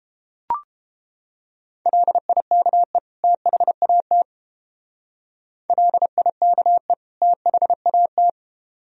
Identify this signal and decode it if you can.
Morse code, heard as a keyed tone, about 31 wpm, 700 Hz. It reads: LIKE THAT LIKE THAT